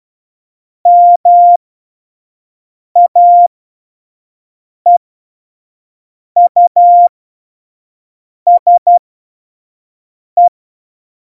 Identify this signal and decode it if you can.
Morse code, heard as a keyed tone, about 12 wpm, spaced as for 5 wpm, 700 Hz. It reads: MAEUSE
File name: Morsekurs / Witze / Witz 03 April 2018, 25 F70